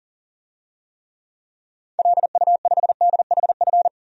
Morse code, heard as a keyed tone, about 40 wpm, 700 Hz. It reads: LU5DHF
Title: DXCC-100 Call Signs - 6 Characters - Repeated 3x - 40wpm